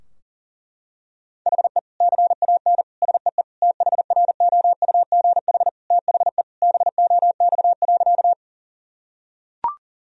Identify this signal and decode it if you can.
Morse code, heard as a keyed tone, about 40 wpm, 700 Hz. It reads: HE CAN SEE THROUGH THE BOX.
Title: Sentences from Top 1000 Words - Review Part 2 - 40wpm